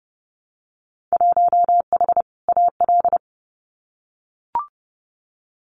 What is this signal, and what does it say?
Morse code, heard as a keyed tone, about 31 wpm, 700 Hz. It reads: 1H AL
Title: Winter Field Day Exchanges 30wpm